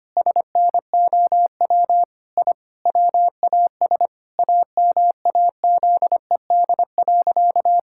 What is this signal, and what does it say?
Morse code, heard as a keyed tone, about 25 wpm, 700 Hz. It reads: SNOW I WAS AMAZED.
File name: practice / 25wpm / Sentences with 4-Word Phrases - 25wpm